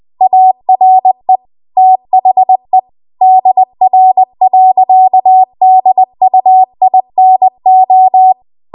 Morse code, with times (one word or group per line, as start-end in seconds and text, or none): 0.2-1.4 ARE
1.8-2.8 THE
3.2-8.3 DR.DUINO